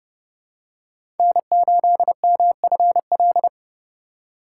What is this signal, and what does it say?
Morse code, heard as a keyed tone, about 30 wpm, 700 Hz. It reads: N8MFL
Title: USA Call Signs - 5 Characters - 2x Character Spacing - 30wpm